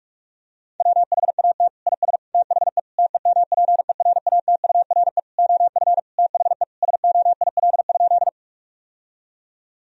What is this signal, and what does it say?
Morse code, heard as a keyed tone, about 46 wpm, 700 Hz. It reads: WHAT IS THE TEMPERATURE OF THE SOIL?
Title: Sentences from Top 1000 Words - Review Part 4 - 45wpm